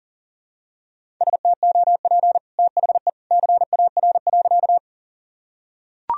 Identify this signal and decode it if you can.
Morse code, heard as a keyed tone, about 40 wpm, 700 Hz. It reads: STOP THE CAR.
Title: Sentences from Top 1000 Words - Review Part 1 - 40wpm